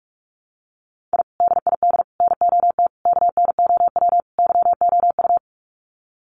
Morse code, heard as a keyed tone, about 44 wpm, 700 Hz. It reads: I DID NOT KNOW YOU
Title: Sets of 5 Words Encore2 45wpm